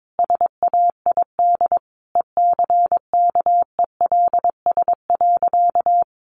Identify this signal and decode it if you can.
Morse code, heard as a keyed tone, about 22 wpm, 700 Hz. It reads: SAID ECKELS.
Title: A Sound Of Thunder 22wpm